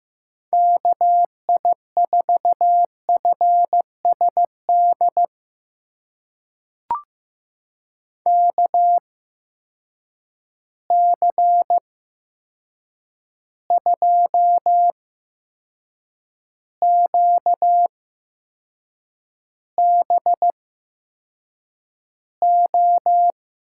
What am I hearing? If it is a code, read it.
Morse code, heard as a keyed tone, about 15 wpm, spaced as for 6 wpm, 700 Hz. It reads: KI4FSD KC2QBO
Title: USA Call Signs - 6 Characters - 8x Character Spacing - 15wpm